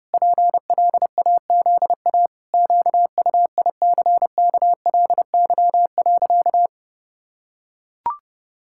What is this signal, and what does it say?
Morse code, heard as a keyed tone, about 32 wpm, 700 Hz. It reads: PLAZA QUICKLY.
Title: Pangram Challenge v2 - 30wpm